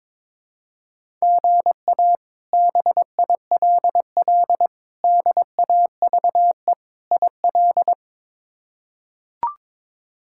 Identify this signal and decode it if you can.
Morse code, heard as a keyed tone, about 22 wpm, 700 Hz. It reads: GA BILL DAVE IL
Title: K1USN SST QSO Practice - 22wpm